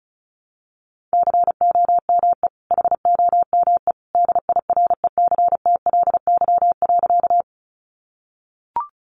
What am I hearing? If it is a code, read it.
Morse code, heard as a keyed tone, about 34 wpm, 700 Hz. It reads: COME HOME DIRECTLY.
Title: Sentences from Top 800 Words 35wpm